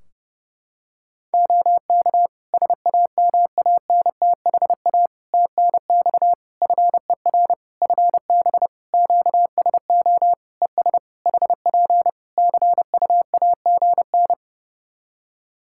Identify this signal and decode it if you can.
Morse code, heard as a keyed tone, about 30 wpm, 700 Hz. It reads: OK SAMANTHA TNX FER FB QSO ES HP CUAGN
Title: Practice QSO - Exchange 4 - Ending - 30wpm